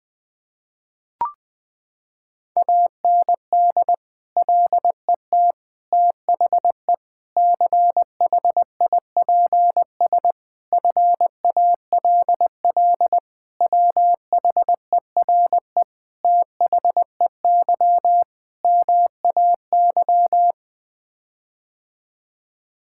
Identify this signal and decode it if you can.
Morse code, heard as a keyed tone, about 21 wpm, 700 Hz. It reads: AND LET THE CHIPS FALL WHERE THEY MAY
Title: Sets of 8 Words - 20wpm